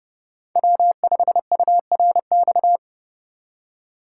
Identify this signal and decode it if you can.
Morse code, heard as a keyed tone, about 30 wpm, 700 Hz. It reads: W5URX
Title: USA Call Signs - 5 Characters - 4x Character Spacing - 30wpm